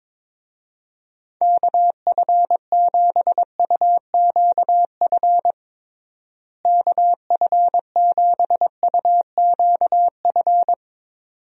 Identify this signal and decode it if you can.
Morse code, heard as a keyed tone, about 22 wpm, 700 Hz. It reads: KF7UQF KF7UQF
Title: USA Call Signs - 6 Characters - Repeated 2x - 22wpm